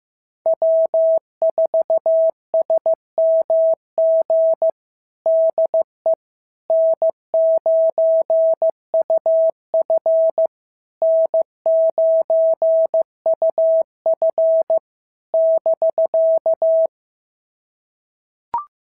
Morse code, heard as a keyed tone, about 14 wpm, 650 Hz. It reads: W4SMG DE N9UF N9UF <BK>
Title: SOTA QSO Practice 15wpm